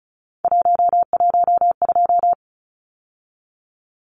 Morse code, heard as a keyed tone, about 34 wpm, 700 Hz. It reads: 112
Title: Numbers - 3-Digit 35wpm